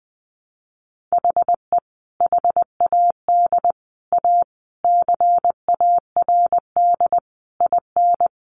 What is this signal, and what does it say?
Morse code, heard as a keyed tone, about 19 wpm, 700 Hz. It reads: HE HAD A CARD IN
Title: Sentences from Top 600 Words 20wpm